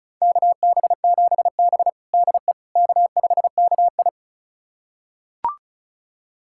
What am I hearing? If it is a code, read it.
Morse code, heard as a keyed tone, about 36 wpm, 700 Hz. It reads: KB7B DE K5KI
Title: USA Callsign DE Callsign - 4 Characters - 35wpm